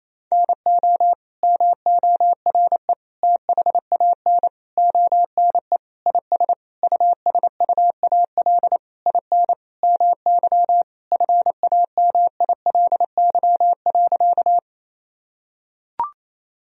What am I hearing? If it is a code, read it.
Morse code, heard as a keyed tone, about 28 wpm, 700 Hz. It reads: NO MORE THAN ONE IS USUAL IN MY FAMILY.
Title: Sentences from Top 1000 Words - Review Part 2 - 28wpm